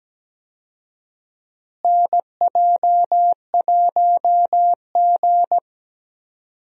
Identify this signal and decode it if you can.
Morse code, heard as a keyed tone, about 17 wpm, 700 Hz. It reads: NJ1G
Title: USA Call Signs - 4 Characters - Repeated 2x - 17wpm